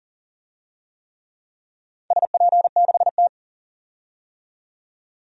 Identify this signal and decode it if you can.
Morse code, heard as a keyed tone, about 40 wpm, 700 Hz. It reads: SP6T
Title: DXCC-100 Call Signs - 4 Characters - 40wpm